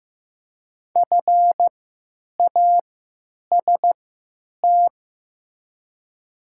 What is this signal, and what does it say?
Morse code, heard as a keyed tone, about 15 wpm, spaced as for 9 wpm, 700 Hz. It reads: FAST
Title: 3x Character Spacing - Top 500 Words - 15wpm